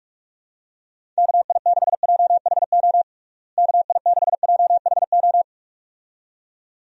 Morse code, heard as a keyed tone, about 46 wpm, 700 Hz. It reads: KI6JHO KI6JHO
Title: USA Call Signs - 6 Characters - Repeated 2x - 45wpm